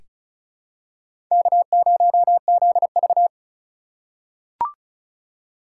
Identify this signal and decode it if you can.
Morse code, heard as a keyed tone, about 34 wpm, 700 Hz. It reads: K0ZV
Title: USA Call Signs - 4 Characters - 6x Character Spacing - 35wpm